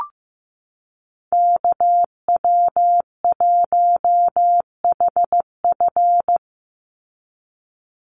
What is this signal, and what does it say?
Morse code, heard as a keyed tone, about 15 wpm, 700 Hz. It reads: KW1HF
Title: USA Call Signs 15wpm